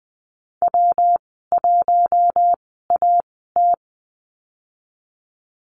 Morse code, heard as a keyed tone, about 20 wpm, spaced as for 15 wpm, 700 Hz. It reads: W1AT
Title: 167 Lesson 27g - Number 7 - Call Signs 20wpm